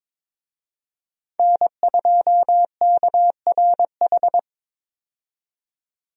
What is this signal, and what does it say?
Morse code, heard as a keyed tone, about 21 wpm, 700 Hz. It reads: N2KRH
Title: USA Call Signs - 5 Characters - 22wpm